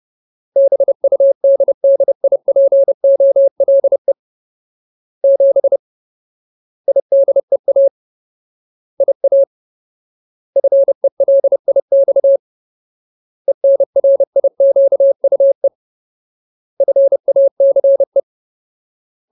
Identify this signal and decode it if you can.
Morse code, heard as a keyed tone, about 30 wpm, 550 Hz. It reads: BUDDIPOLE 7 IDEA IA FELIX ENRIQUE FACE
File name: Ultimate Word List / 30WPM Ultimate